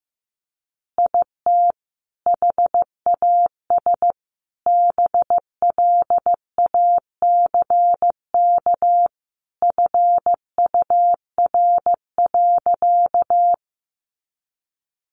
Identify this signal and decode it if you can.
Morse code, heard as a keyed tone, about 15 wpm, 700 Hz. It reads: IT HAS BLACK FUR.
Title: Easy Phrases 15wpm